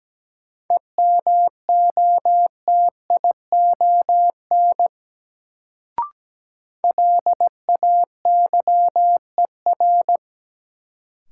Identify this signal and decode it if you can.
Morse code, heard as a keyed tone, about 17 wpm, 700 Hz. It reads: EMOTION LAYER
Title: Sentences from Top 1600 Words - New Word Drill Alt - 17wpm